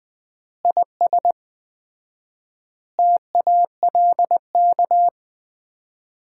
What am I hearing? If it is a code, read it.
Morse code, heard as a keyed tone, about 20 wpm, 700 Hz. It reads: IS TALK
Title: 4x Word-Spacing - Sets of 5 Words - 20wpm